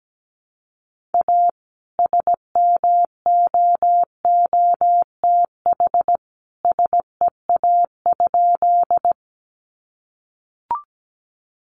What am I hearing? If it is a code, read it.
Morse code, heard as a keyed tone, about 17 wpm, 700 Hz. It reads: A SMOOTH SEA?
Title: Sentences from Top 1400 Words 17wpm